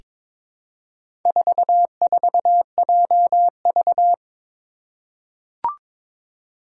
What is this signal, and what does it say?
Morse code, heard as a keyed tone, about 22 wpm, 700 Hz. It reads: 44JV
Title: Four Challenging Letter-Number Combinations 22wpm